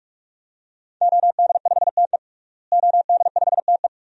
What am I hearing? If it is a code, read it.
Morse code, heard as a keyed tone, about 46 wpm, 700 Hz. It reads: OD5TE OD5TE
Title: DXCC-100 Call Signs - 5 Characters - Repeated 2x - 45wpm